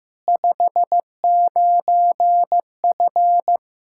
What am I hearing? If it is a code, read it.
Morse code, heard as a keyed tone, about 15 wpm, 700 Hz. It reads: 59F
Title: Practice QSO - Exchange 2 - Info - 15wpm